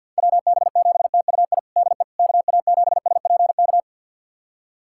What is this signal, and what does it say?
Morse code, heard as a keyed tone, about 52 wpm, 700 Hz. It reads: WB7TUI DE KA7SPK
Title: USA Callsign DE Callsign 50wpm